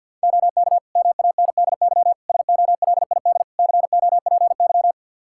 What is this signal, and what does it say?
Morse code, heard as a keyed tone, about 46 wpm, 700 Hz. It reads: OK MANDY SOLID COPY
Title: Practice QSO - Exchange 3 - More Info - 50wpm